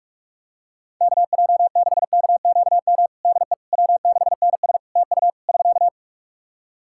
Morse code, heard as a keyed tone, about 47 wpm, 700 Hz. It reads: KJ6KQM DE W6NS TU <SK>
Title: Practice QSO - Exchange 4 - Ending - 45wpm